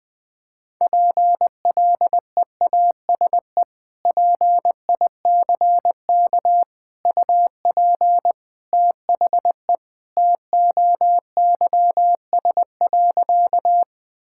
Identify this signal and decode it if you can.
Morse code, heard as a keyed tone, about 20 wpm, 700 Hz. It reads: PLEASE PICK UP THE TOYS.
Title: Sentences with 2-Word Phrases - 20wpm